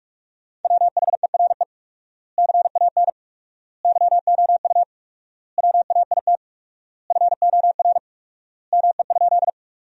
Morse code, heard as a keyed tone, about 47 wpm, 700 Hz. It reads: WHERE CAN YOU WAIT FOR ME?
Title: 4x Word-Spacing - Sentences from Top 500 Words - 45wpm